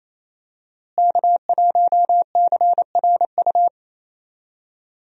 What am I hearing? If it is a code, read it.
Morse code, heard as a keyed tone, about 27 wpm, 700 Hz. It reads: K1CRU
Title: USA Call Signs - 5 Characters - Repeated 3x - 28wpm